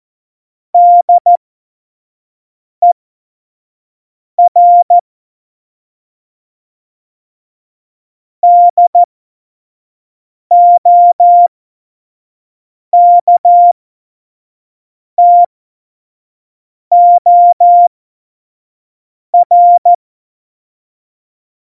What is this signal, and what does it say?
Morse code, heard as a keyed tone, about 14 wpm, spaced as for 5 wpm, 700 Hz. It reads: DER DOKTOR
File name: Morsekurs / Witze / Witz 12. Dezember 2017, 25 F70